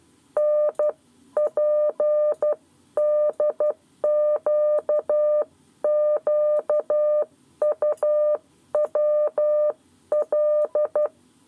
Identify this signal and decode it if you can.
Morse code, heard as a keyed tone, about 12 wpm, 600 Hz. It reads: NPDQQUWL
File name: 7 Lezione cw 1